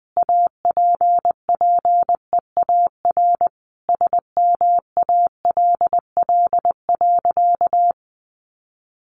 Morse code, heard as a keyed tone, about 20 wpm, 700 Hz. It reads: APPEAR SMALL.
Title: Sentences from Top 1500 Words 20wpm